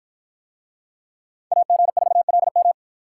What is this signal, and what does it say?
Morse code, heard as a keyed tone, about 53 wpm, 700 Hz. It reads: AG4LM